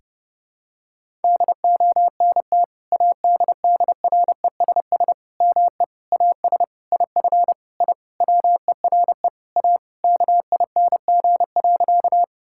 Morse code, heard as a keyed tone, about 30 wpm, 700 Hz. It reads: DONT ADDRESS ME AS IF I WERE A KING.
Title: Sentences from Top 1000 Words - Review Part 6 - 30wpm